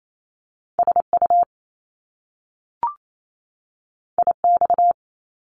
Morse code, heard as a keyed tone, about 28 wpm, 700 Hz. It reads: SU IX
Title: Top Two Letter Combinations in Words 28wpm